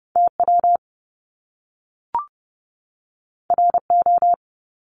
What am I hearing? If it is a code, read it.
Morse code, heard as a keyed tone, about 30 wpm, 700 Hz. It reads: TW RO
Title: Top Two Letter Combinations in Words 30wpm